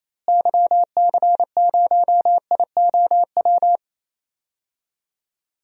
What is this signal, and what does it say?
Morse code, heard as a keyed tone, about 28 wpm, 700 Hz. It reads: YC0IOW
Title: DXCC-100 Call Signs - 6 Characters - Repeated 3x - 28wpm